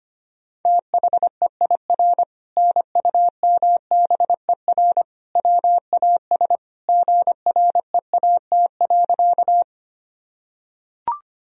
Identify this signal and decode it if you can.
Morse code, heard as a keyed tone, about 25 wpm, 700 Hz. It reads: THEIR NUMBER WAS GREAT.
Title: Sentences from Top 1000 Words - Review Part 1 - 25wpm